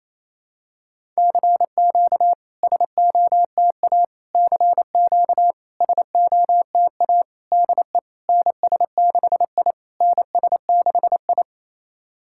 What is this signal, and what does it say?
Morse code, heard as a keyed tone, about 28 wpm, 700 Hz. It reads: CQ SOTA CQ SOTA DE NS6I NS6I
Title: SOTA QSO Practice 28wpm